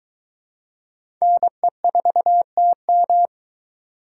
Morse code, heard as a keyed tone, about 21 wpm, 700 Hz. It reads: NE4TM